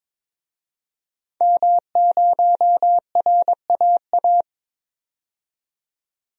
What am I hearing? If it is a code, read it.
Morse code, heard as a keyed tone, about 22 wpm, 700 Hz. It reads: M0RAA
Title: DXCC-100 Call Signs - 5 Characters - 22wpm